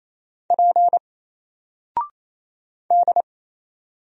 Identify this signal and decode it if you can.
Morse code, heard as a keyed tone, about 27 wpm, 700 Hz. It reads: P D
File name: Single-Letter-Number-28wpm